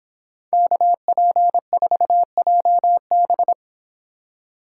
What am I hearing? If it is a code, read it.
Morse code, heard as a keyed tone, about 26 wpm, 700 Hz. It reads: KP4JB